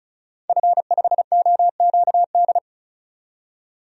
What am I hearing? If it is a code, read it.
Morse code, heard as a keyed tone, about 36 wpm, 700 Hz. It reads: F5OQD
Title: DXCC-100 Call Signs - 5 Characters - 35wpm